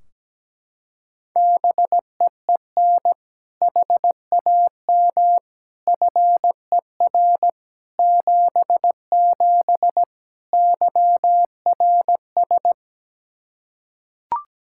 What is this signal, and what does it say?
Morse code, heard as a keyed tone, about 17 wpm, 700 Hz. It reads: BEEN HAM FER 77 YRS
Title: Practice QSO - Exchange 3 - More Info - 17wpm